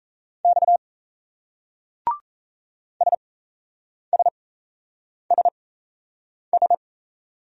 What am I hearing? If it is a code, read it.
Morse code, heard as a keyed tone, about 35 wpm, spaced as for 8 wpm, 700 Hz. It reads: X SSSS
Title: Speed-Racing Single Letters 28wpm